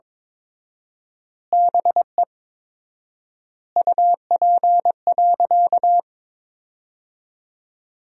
Morse code, heard as a keyed tone, about 22 wpm, 700 Hz. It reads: BE UP.
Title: 4x Word-Spacing - Sentences from Top 300 Words - 22wpm